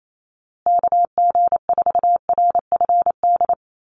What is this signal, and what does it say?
Morse code, heard as a keyed tone, about 28 wpm, 690 Hz. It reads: KG4RFD
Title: USA Call Signs Repeated 3x 28wpm